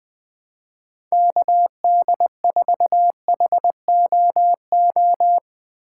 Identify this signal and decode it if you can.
Morse code, heard as a keyed tone, about 20 wpm, 700 Hz. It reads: KD4HOO
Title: USA Call Signs - 2x Character Spacing - 20wpm